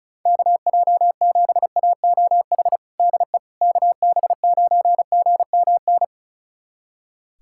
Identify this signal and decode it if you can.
Morse code, heard as a keyed tone, about 34 wpm, 700 Hz. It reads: KJ7AOH DE KB9GMN
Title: USA Callsign DE Callsign - 6 Characters - 35wpm